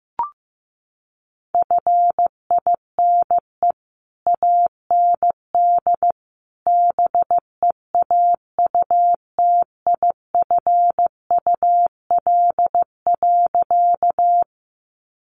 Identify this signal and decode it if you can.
Morse code, heard as a keyed tone, about 15 wpm, 700 Hz. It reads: FINE AND BEAUTIFUL.
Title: Sentences from Top 900 Words 15wpm